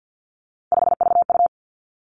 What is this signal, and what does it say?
Morse code, heard as a keyed tone, about 50 wpm, 700 Hz. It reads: 5VU
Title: Three Challenging Letter-Number Combinations 50wpm